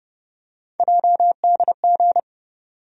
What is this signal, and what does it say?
Morse code, heard as a keyed tone, about 30 wpm, 700 Hz. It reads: JDG
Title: USA Call Sign Postfixes - 3 Letters 30wpm